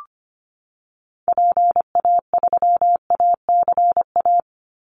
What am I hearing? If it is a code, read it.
Morse code, heard as a keyed tone, about 25 wpm, 700 Hz. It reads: PA3ACA
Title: Call Signs - 25wpm